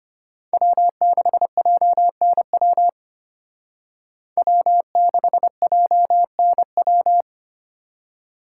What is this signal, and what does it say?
Morse code, heard as a keyed tone, about 32 wpm, 700 Hz. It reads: W6JNW W6JNW